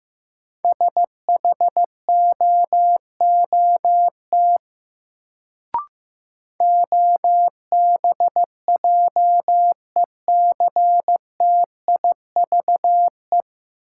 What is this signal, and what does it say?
Morse code, heard as a keyed tone, about 15 wpm, 700 Hz. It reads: SHOOT OBJECTIVE
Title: Sentences from Top 1200 Words - New Word Drill Alt - 15wpm